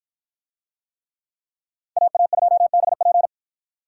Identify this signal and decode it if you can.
Morse code, heard as a keyed tone, about 53 wpm, 700 Hz. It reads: AA2BP